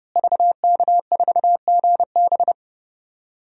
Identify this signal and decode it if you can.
Morse code, heard as a keyed tone, about 30 wpm, 700 Hz. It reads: VK4GB
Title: DXCC-100 Call Signs - 5 Characters - Repeated 2x - 30wpm